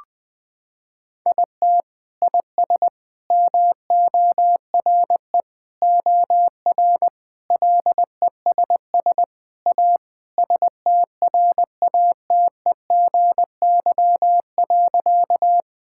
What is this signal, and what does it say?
Morse code, heard as a keyed tone, about 21 wpm, 700 Hz. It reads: IT IS MORE OR LESS A STRATEGY.
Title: Sentences from Top 1000 Words - Review Part 5 - 20wpm